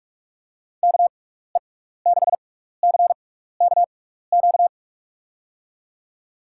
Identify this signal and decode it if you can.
Morse code, heard as a keyed tone, about 46 wpm, spaced as for 16 wpm, 700 Hz. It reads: KE6CKQ
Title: USA Call Signs - 6 Characters - 6x Character Spacing - 45wpm